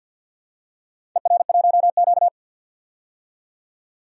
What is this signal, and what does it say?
Morse code, heard as a keyed tone, about 49 wpm, 700 Hz. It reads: ER1Q
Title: DXCC-100 Call Signs - 4 Characters - 50wpm